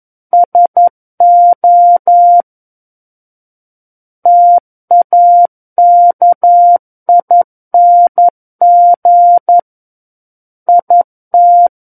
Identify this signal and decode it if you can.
Morse code, heard as a keyed tone, about 11 wpm, 700 Hz. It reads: SO TAKING IT
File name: 11 WPM